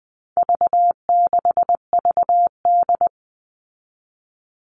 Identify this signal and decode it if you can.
Morse code, heard as a keyed tone, about 20 wpm, 700 Hz. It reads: V6VD
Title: All Cut Numbers - 4-Digit 20wpm